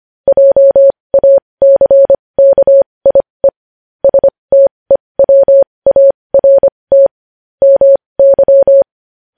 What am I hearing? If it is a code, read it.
Morse code, heard as a keyed tone, about 25 wpm, spaced as for 20 wpm, 550 Hz. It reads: JACKIE STEWART MY
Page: news headlines in morse code at 20 wpm